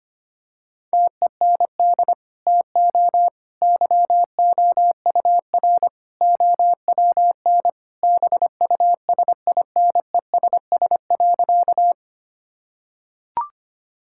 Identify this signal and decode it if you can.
Morse code, heard as a keyed tone, about 25 wpm, 700 Hz. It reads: TEND TO YOUR OWN BUSINESS.
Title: Sentences from Top 1000 Words - Review Part 5 - 25wpm